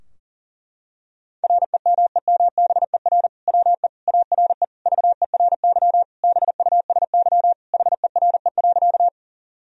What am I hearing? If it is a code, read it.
Morse code, heard as a keyed tone, about 40 wpm, 700 Hz. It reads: REMEMBER WE ARE VERY BUSY HERE.
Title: Sentences from Top 1000 Words - Review Part 3 - 40wpm